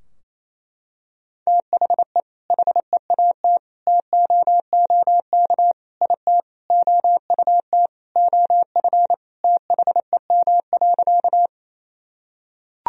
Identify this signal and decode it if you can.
Morse code, heard as a keyed tone, about 27 wpm, 700 Hz. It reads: THE HEAT TOOK IT OUT OF THEM.
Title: Sentences from Top 1000 Words - Review Part 3 - 28wpm